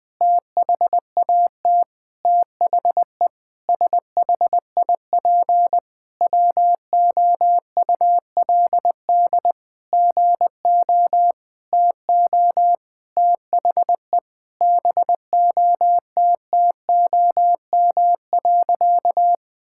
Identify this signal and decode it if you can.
Morse code, heard as a keyed tone, about 20 wpm, 700 Hz. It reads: THAT THE SHIP WOULD GO TO THE BOTTOM.
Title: Sentences from Top 1000 Words - Review Part 6 - 20wpm